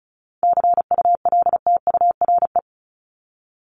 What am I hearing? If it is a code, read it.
Morse code, heard as a keyed tone, about 34 wpm, 700 Hz. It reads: CULTURE
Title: Top 751-1000 Words 35wpm